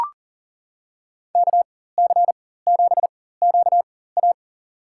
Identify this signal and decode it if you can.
Morse code, heard as a keyed tone, about 40 wpm, spaced as for 19 wpm, 700 Hz. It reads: KC7QA
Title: USA Call Signs - 5 Characters - 4x Character Spacing - 40wpm